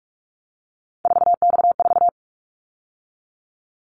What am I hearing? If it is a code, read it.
Morse code, heard as a keyed tone, about 46 wpm, 700 Hz. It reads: 4X4
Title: Callsign Prefixes for Top 50 DXCC - 45wpm